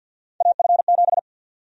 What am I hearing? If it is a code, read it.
Morse code, heard as a keyed tone, about 50 wpm, 700 Hz. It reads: AF7
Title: USA Call Sign Prefixes - 50wpm